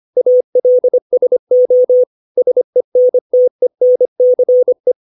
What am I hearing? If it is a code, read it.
Morse code, heard as a keyed tone, about 25 wpm, 500 Hz. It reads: ALSO SENTENCE
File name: 632 words / 632 Words 25WPM